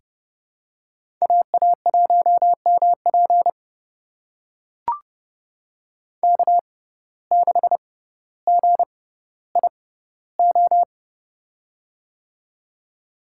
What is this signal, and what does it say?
Morse code, heard as a keyed tone, about 30 wpm, spaced as for 15 wpm, 700 Hz. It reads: AA1MP K6GIO
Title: USA Call Signs - 5 Characters - 6x Character Spacing - 30wpm